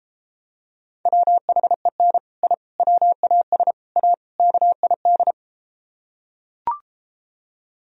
Morse code, heard as a keyed tone, about 33 wpm, 700 Hz. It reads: WHEN I WAS A KID